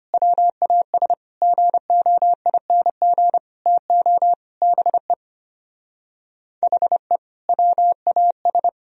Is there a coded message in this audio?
Yes: Morse code from 0.1 to 8.7 s, heard as a keyed tone, about 33 wpm, 700 Hz. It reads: WAS GOING TO BE HE WAS